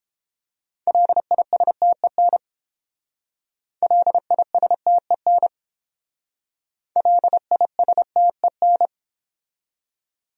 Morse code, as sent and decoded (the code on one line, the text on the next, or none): .-.. .. ... - . -. / .-.. .. ... - . -. / .-.. .. ... - . -.
LISTEN LISTEN LISTEN